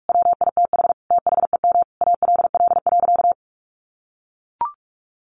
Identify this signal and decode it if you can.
Morse code, heard as a keyed tone, about 44 wpm, 700 Hz. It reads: WITH THEM ALL.
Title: Sentences from Top 1400 Words 45wpm